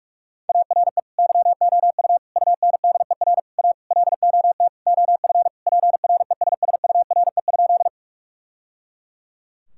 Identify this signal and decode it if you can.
Morse code, heard as a keyed tone, about 46 wpm, 700 Hz. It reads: ARE YOU UNDER A LOT OF PRESSURE?
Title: Sentences from Top 1000 Words - Review Part 6 - 45wpm